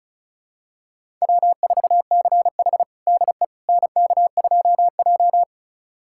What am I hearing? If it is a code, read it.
Morse code, heard as a keyed tone, about 36 wpm, 700 Hz. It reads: W4CH DE NK2J
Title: USA Callsign DE Callsign - 4 Characters - 35wpm